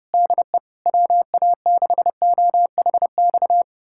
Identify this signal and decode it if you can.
Morse code, heard as a keyed tone, about 30 wpm, 700 Hz. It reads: DE WA6OHX
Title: USA Callsign DE Callsign - 6 Characters - 30wpm